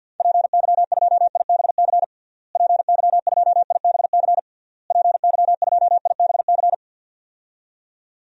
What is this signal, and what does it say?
Morse code, heard as a keyed tone, about 52 wpm, 700 Hz. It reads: PY2IBC PY2IBC PY2IBC
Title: DXCC-100 Call Signs - 6 Characters - Repeated 3x - 50wpm